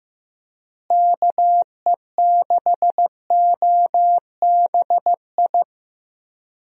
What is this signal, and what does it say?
Morse code, heard as a keyed tone, about 16 wpm, 700 Hz. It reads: KE6OBI
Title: USA Call Signs - 6 Characters - Repeated 3x - 15wpm